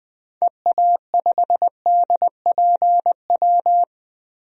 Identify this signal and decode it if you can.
Morse code, heard as a keyed tone, about 20 wpm, 700 Hz. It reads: EA5DPW
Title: DXCC-100 Call Signs - 6 Characters - 20wpm